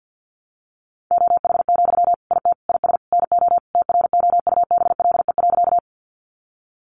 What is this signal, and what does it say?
Morse code, heard as a keyed tone, about 50 wpm, 700 Hz. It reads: OH, IT IS NO TROUBLE.
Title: Sentences from Top 1200 Words 50wpm